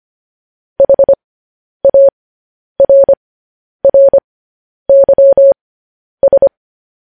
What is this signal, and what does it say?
Morse code, heard as a keyed tone, about 25 wpm, spaced as for 10 wpm, 550 Hz. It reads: HARRYS
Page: news-headlines-in-morse-code-at-10-wpm